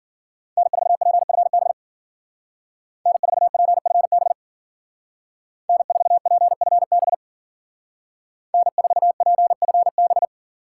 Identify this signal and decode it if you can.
Morse code, heard as a keyed tone, about 49 wpm, 700 Hz. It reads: N4PFB N4PFB N4PFB N4PFB
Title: Speed Racing - USA Call Signs 40wpm